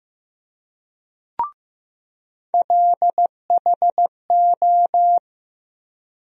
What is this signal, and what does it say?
Morse code, heard as a keyed tone, about 15 wpm, 700 Hz. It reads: LHO
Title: USA Call Sign Postfixes - 3 Letters 15wpm